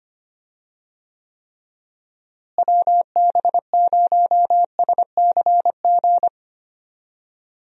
Morse code, heard as a keyed tone, about 25 wpm, 700 Hz. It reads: WB0SCG
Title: USA Call Signs - 6 Characters - Repeated 2x - 25wpm